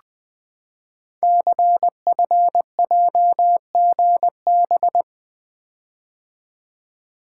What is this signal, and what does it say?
Morse code, heard as a keyed tone, about 20 wpm, 700 Hz. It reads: CFJGB
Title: Code Groups - 5 Letters - 20wpm